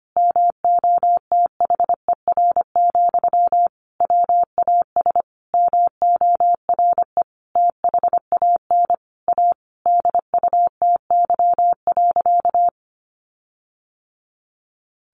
Morse code, heard as a keyed tone, about 25 wpm, 700 Hz. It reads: MOTHER, WAS MORE THAN A DUTY.
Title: Sentences from Top 1500 Words 25wpm